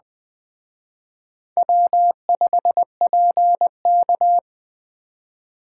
Morse code, heard as a keyed tone, about 20 wpm, 700 Hz. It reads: W5PK